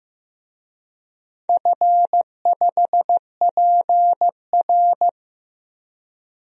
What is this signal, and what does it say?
Morse code, heard as a keyed tone, about 15 wpm, 700 Hz. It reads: F5PR
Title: DXCC-100 Call Signs - 4 Characters - Repeated 3x - 15wpm